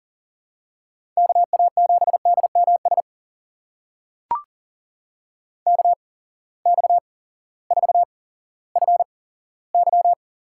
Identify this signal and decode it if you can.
Morse code, heard as a keyed tone, about 40 wpm, spaced as for 18 wpm, 700 Hz. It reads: KA7DMS KX4FY
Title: USA Call Signs - 8x Character Spacing - 40wpm